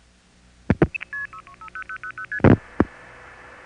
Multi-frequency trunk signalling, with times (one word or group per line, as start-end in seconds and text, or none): 1.3-1.4 6
1.7-2.4 0000ST